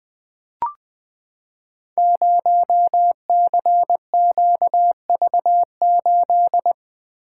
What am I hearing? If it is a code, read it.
Morse code, heard as a keyed tone, about 20 wpm, 700 Hz. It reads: 0CQV8
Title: Code Groups - 5 Letter:Numbers - 20wpm